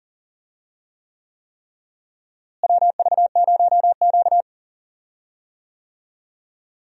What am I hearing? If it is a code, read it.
Morse code, heard as a keyed tone, about 40 wpm, 700 Hz. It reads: WV0Q
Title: USA Call Signs - 4 Characters - Repeated 3x - 40wpm